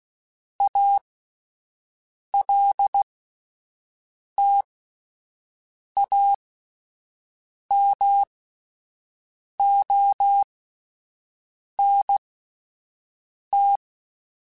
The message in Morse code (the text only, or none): ALTAMONT